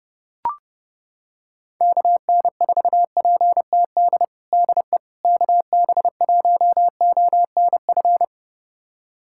Morse code, heard as a keyed tone, about 30 wpm, 700 Hz. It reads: KN4PTD DE KB1ONF
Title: USA Callsign DE Callsign - 6 Characters - 30wpm